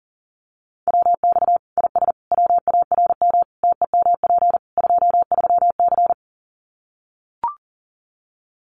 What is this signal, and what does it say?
Morse code, heard as a keyed tone, about 40 wpm, 700 Hz. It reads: WX IS WARM TEMP 23C
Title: Common QSO Phrases 40wpm